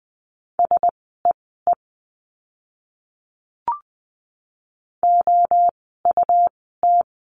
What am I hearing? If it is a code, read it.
Morse code, heard as a keyed tone, about 20 wpm, spaced as for 14 wpm, 700 Hz. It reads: SEE OUT
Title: 230 Lesson 37f - Letter x - Words 20wpm